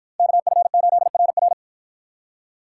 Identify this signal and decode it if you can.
Morse code, heard as a keyed tone, about 53 wpm, 700 Hz. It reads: KF8RR